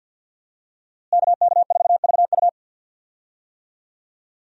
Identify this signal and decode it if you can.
Morse code, heard as a keyed tone, about 52 wpm, 700 Hz. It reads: KK4VU
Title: USA Call Signs - 5 Characters - Repeated 2x - 50wpm